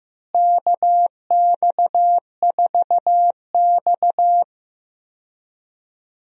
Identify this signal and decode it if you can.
Morse code, heard as a keyed tone, about 15 wpm, 700 Hz. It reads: KX4X
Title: USA Call Signs - 4 Characters - Repeated 3x - 15wpm